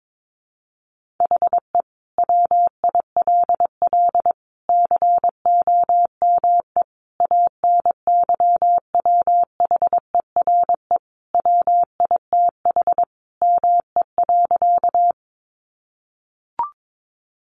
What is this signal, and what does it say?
Morse code, heard as a keyed tone, about 22 wpm, 700 Hz. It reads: HE WILL COME ANYWHERE WITH ME.
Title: Sentences from Top 1400 Words 22wpm